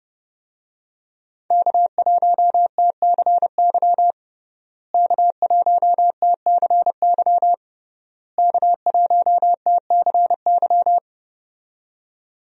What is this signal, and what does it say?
Morse code, heard as a keyed tone, about 30 wpm, 700 Hz. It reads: K1TCY K1TCY K1TCY
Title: USA Call Signs - 5 Characters - Repeated 3x - 30wpm